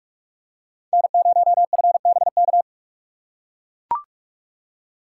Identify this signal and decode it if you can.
Morse code, heard as a keyed tone, about 43 wpm, 690 Hz. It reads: N0FBK
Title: USA Call Signs - 5 Characters - 6x Character Spacing - 45wpm